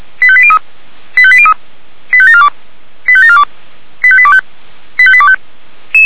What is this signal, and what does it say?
Five-tone selective call (ZVEI), heard as quick runs of five tones, tones of about 70 ms. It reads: 85803 85803 85742 85742 85825 85825